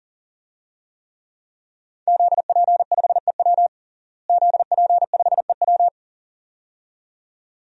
Morse code, heard as a keyed tone, about 41 wpm, 700 Hz. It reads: ZP5EW ZP5EW
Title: DXCC-100 Call Signs - 5 Characters - Repeated 2x - 40wpm